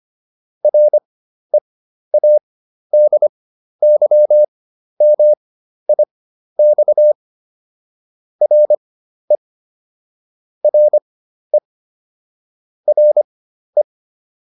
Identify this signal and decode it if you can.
Morse code, heard as a keyed tone, about 25 wpm, spaced as for 12 wpm, 600 Hz. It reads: READYMIX RE RE RE
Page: fundamental-practice-files